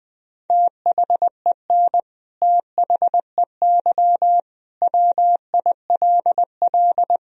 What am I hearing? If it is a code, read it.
Morse code, heard as a keyed tone, about 21 wpm, 700 Hz. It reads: THEN THEY WILL
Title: Sentences from Top 200 Words and Abbreviations 20wpm